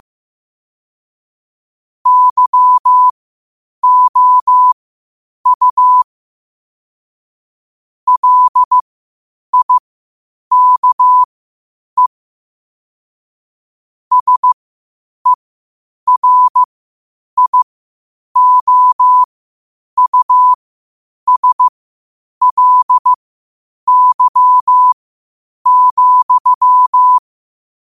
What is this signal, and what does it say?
Morse code, heard as a keyed tone, about 15 wpm, spaced as for 8 wpm, 1000 Hz. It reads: YOU LIKE SERIOUSLY,